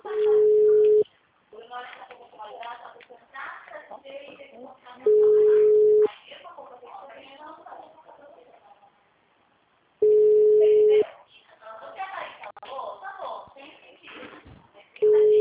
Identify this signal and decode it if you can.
Ringback tone (Europe, 425 Hz), with speech.